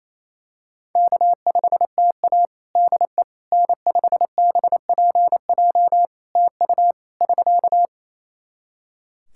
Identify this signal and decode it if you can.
Morse code, heard as a keyed tone, about 27 wpm, 700 Hz. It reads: K5TA DE N5BPJ TU <SK>
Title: K1USN SST QSO Practice 28wpm